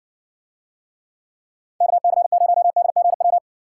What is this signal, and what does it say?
Morse code, heard as a keyed tone, about 64 wpm, 700 Hz. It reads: KC0DGW